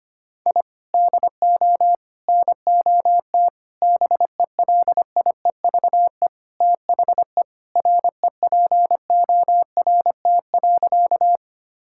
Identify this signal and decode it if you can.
Morse code, heard as a keyed tone, about 25 wpm, 700 Hz. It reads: I DO NOT BELIEVE THE REPORT.
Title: Sentences from Top 1000 Words - Review Part 6 - 25wpm